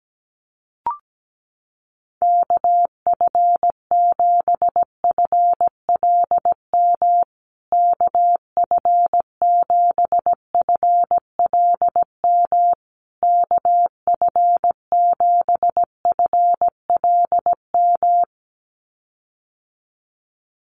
Morse code, heard as a keyed tone, about 17 wpm, 700 Hz. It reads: KF7FLM KF7FLM KF7FLM
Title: USA Call Signs Repeated 3x 17wpm